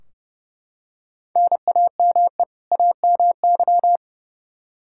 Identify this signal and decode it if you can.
Morse code, heard as a keyed tone, about 30 wpm, 700 Hz. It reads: NAME AMY
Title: Practice QSO - Exchange 1 - Beginning - 30wpm